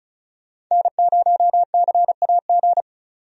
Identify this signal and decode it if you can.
Morse code, heard as a keyed tone, about 36 wpm, 700 Hz. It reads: N0CAG
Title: USA Call Signs - 5 Characters - 35wpm